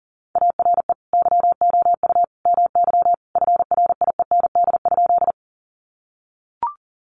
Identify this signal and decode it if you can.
Morse code, heard as a keyed tone, about 40 wpm, 700 Hz. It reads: ARE YOU MY FRIEND?
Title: Sentences from Top 300 Words 40wpm